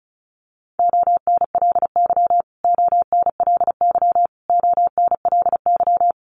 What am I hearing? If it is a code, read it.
Morse code, heard as a keyed tone, about 34 wpm, 700 Hz. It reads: ONLY ONLY ONLY
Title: Top 500 Words Repeated 35wpm